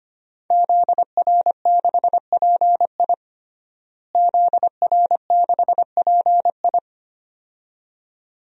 Morse code, heard as a keyed tone, about 25 wpm, 700 Hz. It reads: ZR6PI ZR6PI
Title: DXCC-100 Call Signs - 5 Characters - Repeated 2x - 25wpm